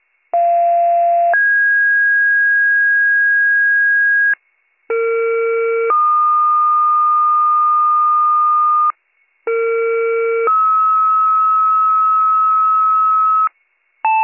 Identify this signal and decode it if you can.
Two-tone page, heard as a one-second tone, then a three-second tone, about 690 then 1650 Hz, about 460 then 1220 Hz, about 460 then 1290 Hz. It is heard from 0.3 to 13.5 s.